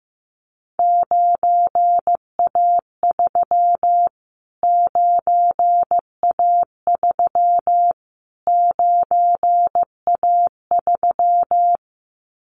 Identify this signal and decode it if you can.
Morse code, heard as a keyed tone, about 15 wpm, 700 Hz. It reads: 9A3 9A3 9A3
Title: Callsign Prefixes for Top 25 DXCC - Repeated 3x - 15wpm